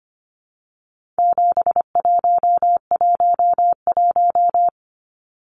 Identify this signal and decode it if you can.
Morse code, heard as a keyed tone, about 25 wpm, 700 Hz. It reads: 7111
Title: Numbers - 4-Digit 25wpm